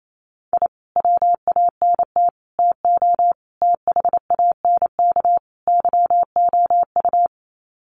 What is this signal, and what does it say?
Morse code, heard as a keyed tone, about 28 wpm, 700 Hz. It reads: I WANT TO THANK YOU
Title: Sets of 5 Words Encore 28wpm